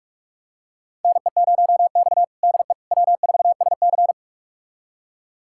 Morse code, heard as a keyed tone, about 41 wpm, 700 Hz. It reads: NE0X DE W4SC
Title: USA Callsign DE Callsign - 4 Characters - 45wpm